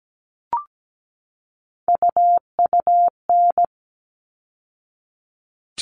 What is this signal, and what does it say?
Morse code, heard as a keyed tone, about 17 wpm, 700 Hz. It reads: UUN
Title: All Cut Numbers - 3-Digit 17wpm